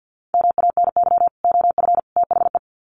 Morse code, heard as a keyed tone, about 50 wpm, 700 Hz. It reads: MANY OF THE
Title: Sets of 3 Words 50wpm